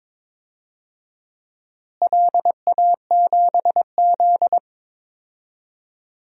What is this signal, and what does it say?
Morse code, heard as a keyed tone, about 22 wpm, 700 Hz. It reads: LA7Z
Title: DXCC-100 Call Signs - 4 Characters - Repeated 3x - 22wpm